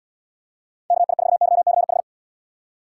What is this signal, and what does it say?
Morse code, heard as a keyed tone, about 74 wpm, 700 Hz. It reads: KI4YCH